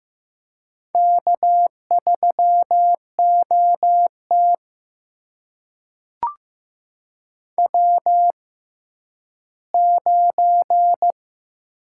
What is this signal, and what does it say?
Morse code, heard as a keyed tone, about 15 wpm, spaced as for 10 wpm, 700 Hz. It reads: K3OT W9
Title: USA Call Signs - 4 Characters - 6x Character Spacing - 15wpm